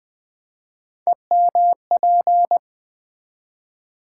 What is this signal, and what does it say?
Morse code, heard as a keyed tone, about 20 wpm, 700 Hz. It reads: EMP
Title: Build A Word - 3 Letters At A Time - 9-Letter Words - 20wpm